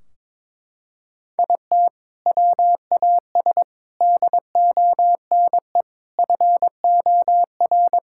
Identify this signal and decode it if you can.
Morse code, heard as a keyed tone, about 21 wpm, 700 Hz. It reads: IT WAS DONE FOR
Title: Sentences from Top 1600 Words 22wpm